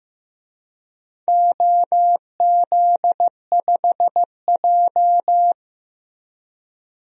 Morse code, heard as a keyed tone, about 15 wpm, 700 Hz. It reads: OZ5J
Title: DXCC-100 Call Signs - 4 Characters - Repeated 2x - 15wpm